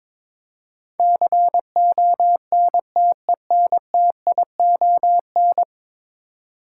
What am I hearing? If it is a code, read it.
Morse code, heard as a keyed tone, about 22 wpm, 700 Hz. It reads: CONTENTION
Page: morse-code-ninja-practice-22wpm